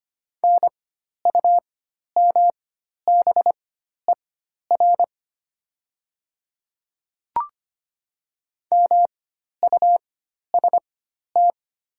Morse code, heard as a keyed tone, about 25 wpm, spaced as for 12 wpm, 700 Hz. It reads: NUMBER MUST
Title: 4x Character Spacing - Top 300 Words - 25wpm